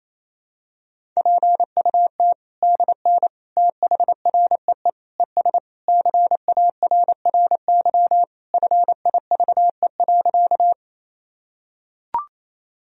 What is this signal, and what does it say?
Morse code, heard as a keyed tone, about 28 wpm, 700 Hz. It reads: PUT DN THREE ES CARRY FIVE.
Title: Sentences from Top 400 Words and Abbreviations 28wpm